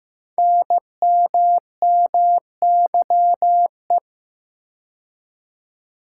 Morse code, heard as a keyed tone, about 15 wpm, 700 Hz. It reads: NMMYE
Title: Code Groups - 5 Letters - 15wpm